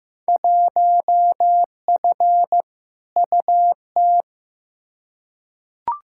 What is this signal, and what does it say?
Morse code, heard as a keyed tone, about 15 wpm, 700 Hz. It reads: 1F UT
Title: Field Day Exchanges Encore - 15wpm